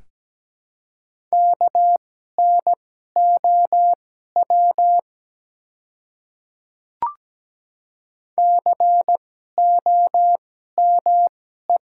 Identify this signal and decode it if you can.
Morse code, heard as a keyed tone, about 17 wpm, spaced as for 12 wpm, 700 Hz. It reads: KNOW COME
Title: 2x Character Spacing - Top 100 Words - 17wpm